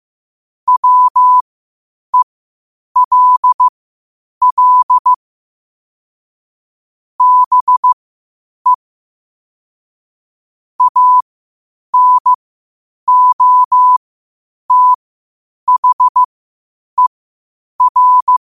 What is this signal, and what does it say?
Morse code, heard as a keyed tone, about 15 wpm, spaced as for 8 wpm, 1000 Hz. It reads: WELL BE ANOTHER
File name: qotdfiles / 8wpm